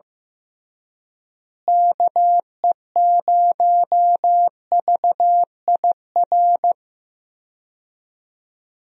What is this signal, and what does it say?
Morse code, heard as a keyed tone, about 15 wpm, 700 Hz. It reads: KE0VIR
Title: USA Call Signs - 6 Characters - Repeated 3x - 15wpm